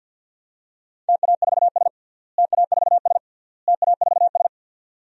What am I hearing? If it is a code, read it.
Morse code, heard as a keyed tone, about 46 wpm, 700 Hz. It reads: TA4S TA4S TA4S
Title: DXCC-100 Call Signs - 4 Characters - Repeated 3x - 50wpm